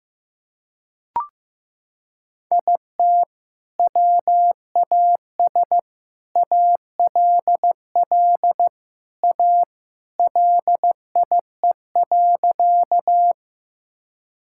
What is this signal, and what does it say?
Morse code, heard as a keyed tone, about 15 wpm, 700 Hz. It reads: IT WAS ALL A LIE.
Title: Sentences from Top 1600 Words 15wpm